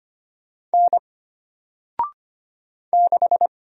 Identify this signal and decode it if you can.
Morse code, heard as a keyed tone, about 25 wpm, 700 Hz. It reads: N 6
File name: Single-Letter-Number-25wpm